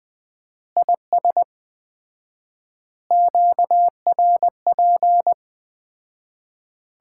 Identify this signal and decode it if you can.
Morse code, heard as a keyed tone, about 20 wpm, 700 Hz. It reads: IS QRP
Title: Common QSO Phrases - 4x Word-Spacing - 20wpm